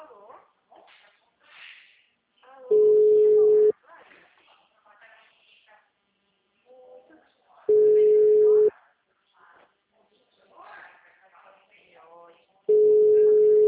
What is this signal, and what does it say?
Ringback tone (Europe, 425 Hz), with speech.